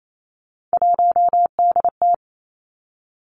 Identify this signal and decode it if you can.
Morse code, heard as a keyed tone, about 28 wpm, 700 Hz. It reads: 1DT
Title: Three Challenging Letter-Number Combinations 28wpm